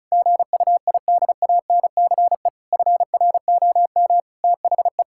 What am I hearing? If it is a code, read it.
Morse code, heard as a keyed tone, about 36 wpm, 690 Hz. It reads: GUIDANCE FROM THE
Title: Sentences from Top 1600 Words 35wpm